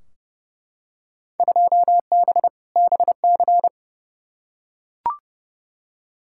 Morse code, heard as a keyed tone, about 31 wpm, 700 Hz. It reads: 2B BC
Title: Field Day Exchanges Encore - 30wpm